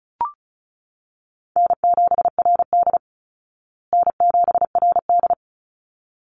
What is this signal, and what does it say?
Morse code, heard as a keyed tone, about 34 wpm, 700 Hz. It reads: N7RD N7RD
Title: Call Signs Repeated 3x - 35wpm